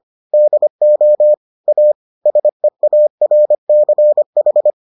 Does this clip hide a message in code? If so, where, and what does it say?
Yes: Morse code from 0.3 to 4.7 s, heard as a keyed tone, about 25 wpm, 600 Hz. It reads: DO A SEARCH